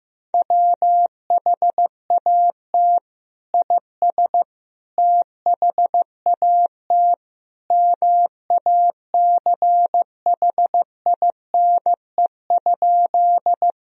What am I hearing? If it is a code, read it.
Morse code, heard as a keyed tone, about 15 wpm, 700 Hz. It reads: WHAT IS THAT MACHINE?
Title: Sentences from Top 1000 Words - Review Part 2 - 15wpm